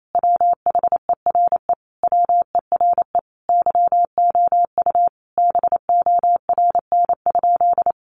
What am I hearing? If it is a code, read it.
Morse code, heard as a keyed tone, about 28 wpm, 700 Hz. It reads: WHERE WERE YOU BORN?
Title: Sentences from Top 1000 Words 28wpm